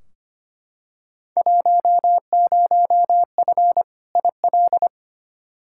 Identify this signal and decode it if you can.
Morse code, heard as a keyed tone, about 25 wpm, 700 Hz. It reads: 10F IL
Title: Field Day Exchanges Encore - 25wpm